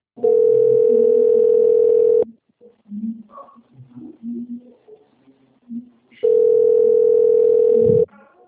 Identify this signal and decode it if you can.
Ringback tone (North America), 440 + 480 Hz.
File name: out- - - - - .21717